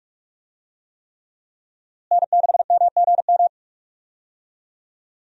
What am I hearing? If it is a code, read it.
Morse code, heard as a keyed tone, about 45 wpm, 700 Hz. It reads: N6MGM
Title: USA Call Signs - 5 Characters - Repeated 3x - 45wpm